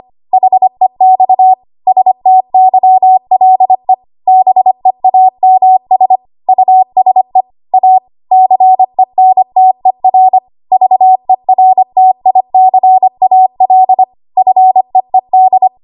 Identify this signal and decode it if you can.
Morse code, heard as a keyed tone, about 25 wpm, 750 Hz. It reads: HEX STYLE BEAMS USE A CENTER VERTICAL FEED
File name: Archive / 25 WPM / 25WPM